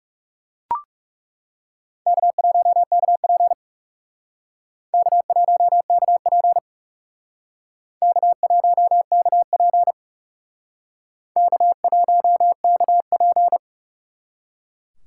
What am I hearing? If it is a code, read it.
Morse code, heard as a keyed tone, about 37 wpm, 700 Hz. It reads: K1KP K1KP K1KP K1KP
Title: Speed Racing - USA Call Signs 30wpm